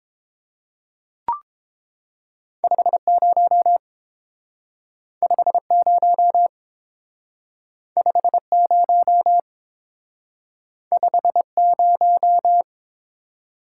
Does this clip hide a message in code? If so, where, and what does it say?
Yes: Morse code from 2.6 to 12.6 s, heard as a keyed tone, about 30 wpm, 700 Hz. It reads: 50 50 50 50